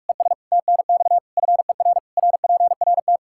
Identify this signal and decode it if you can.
Morse code, heard as a keyed tone, about 41 wpm, 700 Hz. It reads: ES TNX FER RPRT
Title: Practice QSO - Exchange 1 - Beginning - 45wpm